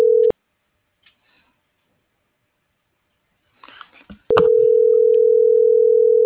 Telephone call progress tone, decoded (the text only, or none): ringback tone